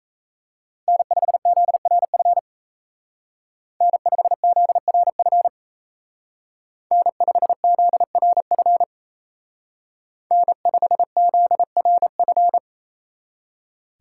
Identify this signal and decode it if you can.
Morse code, heard as a keyed tone, about 37 wpm, 700 Hz. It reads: N5ZRF N5ZRF N5ZRF N5ZRF